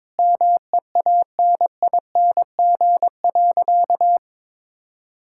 Morse code, heard as a keyed tone, about 22 wpm, 700 Hz. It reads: MEANING.
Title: Sentences from Top 1000 Words - Review Part 6 - 22wpm